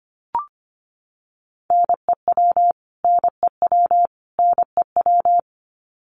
Morse code, heard as a keyed tone, about 25 wpm, 700 Hz. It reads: NEW NEW NEW
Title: Top 500 Words Repeated 25wpm